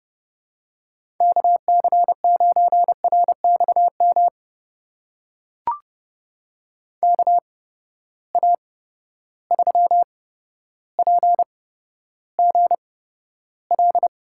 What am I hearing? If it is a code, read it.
Morse code, heard as a keyed tone, about 31 wpm, spaced as for 13 wpm, 700 Hz. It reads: KC9RXM KA3PGL
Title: USA Call Signs - 6 Characters - 8x Character Spacing - 30wpm